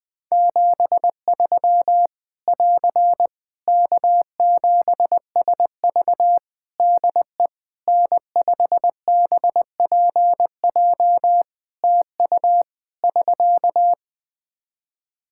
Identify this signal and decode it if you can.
Morse code, heard as a keyed tone, about 20 wpm, 700 Hz. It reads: 73 <AR> K7SV DE N5BPJ TU <SK>
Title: Practice QSO - Exchange 4 - Ending - 20wpm